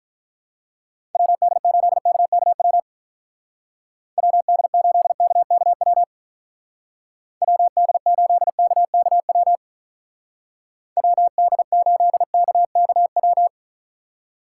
Morse code, heard as a keyed tone, about 44 wpm, 700 Hz. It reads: WD8KKW WD8KKW WD8KKW WD8KKW